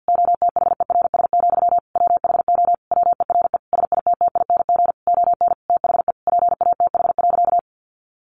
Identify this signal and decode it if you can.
Morse code, heard as a keyed tone, about 48 wpm, 700 Hz. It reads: OTHERS, WHO WERE SITTING ON THE PATH.
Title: A Sound Of Thunder 50wpm